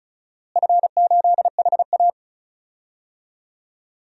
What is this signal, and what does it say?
Morse code, heard as a keyed tone, about 36 wpm, 700 Hz. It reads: F8HA
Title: DXCC-100 Call Signs - 4 Characters - Repeated 2x - 35wpm